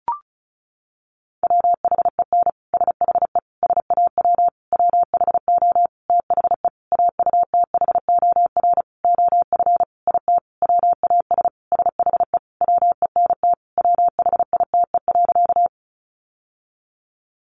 Morse code, heard as a keyed tone, about 34 wpm, 700 Hz. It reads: WHEN SHE SAW WHO THE AUTHOR OF IT WAS SHE WENT WHITE.
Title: Sentences from Top 1200 Words 35wpm